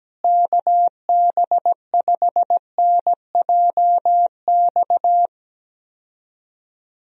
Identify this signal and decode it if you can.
Morse code, heard as a keyed tone, about 17 wpm, 700 Hz. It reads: KB5NJX
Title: USA Call Signs - 6 Characters - Repeated 2x - 17wpm